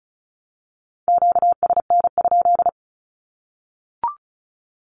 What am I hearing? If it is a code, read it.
Morse code, heard as a keyed tone, about 34 wpm, 700 Hz. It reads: QSN?
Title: Q-Codes Repeated 3x 35wpm